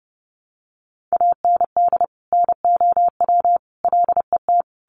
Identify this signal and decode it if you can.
Morse code, heard as a keyed tone, about 30 wpm, 700 Hz. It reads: AND NOW LET
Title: Sentences from Top 300 Words 30wpm